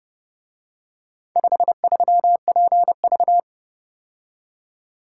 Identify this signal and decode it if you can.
Morse code, heard as a keyed tone, about 29 wpm, 700 Hz. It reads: 53PV